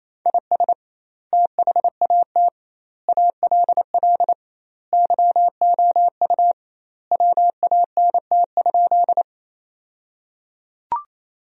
Morse code, heard as a keyed tone, about 28 wpm, 700 Hz. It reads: IS THAT ALL YOU WANT?
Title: 2x Word-Spacing - Sentences from Top 200 Words - 28wpm